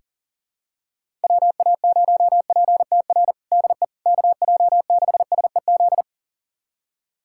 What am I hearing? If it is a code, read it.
Morse code, heard as a keyed tone, about 40 wpm, 700 Hz. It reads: WA0PTR DE KJ6SEZ
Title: USA Callsign DE Callsign - 6 Characters - 40wpm